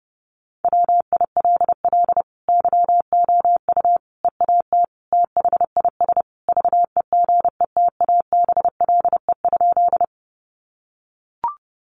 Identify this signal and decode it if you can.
Morse code, heard as a keyed tone, about 30 wpm, 700 Hz. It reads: WILL YOU EAT THIS VEGETABLE?
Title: Sentences from Top 1300 Words 30wpm